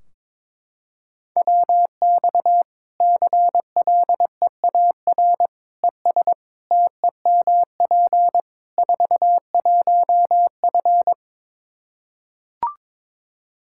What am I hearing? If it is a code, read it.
Morse code, heard as a keyed tone, about 22 wpm, 700 Hz. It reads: WX CLEAR ES TEMP 41F
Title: K1USN SST QSO Practice 22wpm